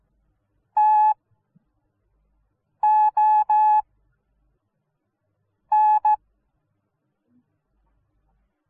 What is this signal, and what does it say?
Morse code, heard as a keyed tone, about 15 wpm, spaced as for 4 wpm, 850 Hz. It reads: TON